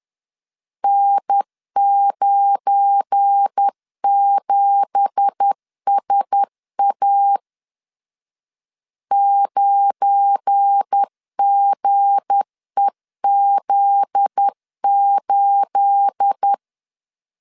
Morse code, heard as a keyed tone, about 10 wpm, 800 Hz. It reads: N97SA 9GEZ8